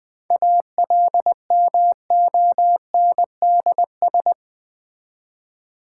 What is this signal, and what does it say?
Morse code, heard as a keyed tone, about 20 wpm, 700 Hz. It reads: ALMONDS
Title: Difficult Words with Hints - 20wpm